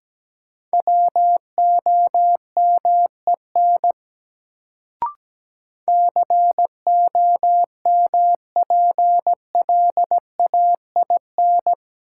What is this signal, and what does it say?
Morse code, heard as a keyed tone, about 17 wpm, 700 Hz. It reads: WOMEN COMPLAIN
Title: Sentences from Top 1300 Words - New Word Drill - 17wpm